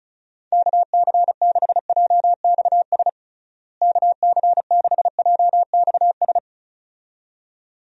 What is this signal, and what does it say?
Morse code, heard as a keyed tone, about 34 wpm, 700 Hz. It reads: KC6JXS KC6JXS
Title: USA Call Signs - 6 Characters - Repeated 2x - 35wpm